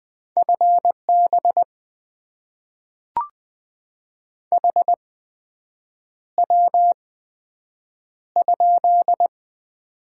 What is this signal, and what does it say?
Morse code, heard as a keyed tone, about 20 wpm, spaced as for 7 wpm, 700 Hz. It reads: FB HW?
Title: Top 20 QSO Elements - 8x Character Spacing - 20wpm